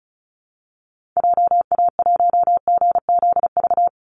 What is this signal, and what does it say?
Morse code, heard as a keyed tone, about 34 wpm, 700 Hz. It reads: JA1GZV
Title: Call Signs Repeated 2x - 35wpm